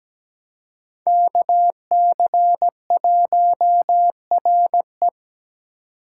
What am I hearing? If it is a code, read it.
Morse code, heard as a keyed tone, about 17 wpm, 700 Hz. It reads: KC1RE
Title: USA Call Signs - 5 Characters - 3x Character Spacing - 17wpm